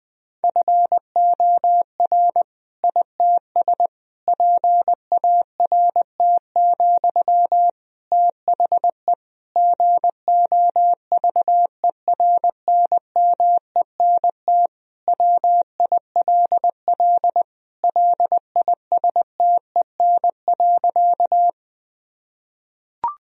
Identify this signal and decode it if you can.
Morse code, heard as a keyed tone, about 20 wpm, 700 Hz. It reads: FOR ITS PART, THE GOVERNMENT WILL LISTEN.
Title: Sentences from Top 1000 Words - Review Part 3 - 20wpm